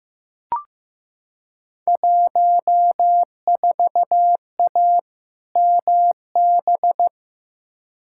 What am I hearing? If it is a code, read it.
Morse code, heard as a keyed tone, about 15 wpm, 700 Hz. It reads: 14A MB
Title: Field Day Exchanges Encore - 15wpm